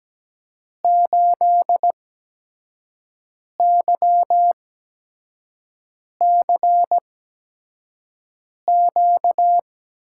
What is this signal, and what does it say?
Morse code, heard as a keyed tone, about 17 wpm, spaced as for 5 wpm, 700 Hz. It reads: 8YCQ
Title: USA Call Signs - 6 Characters - 8x Character Spacing - 17wpm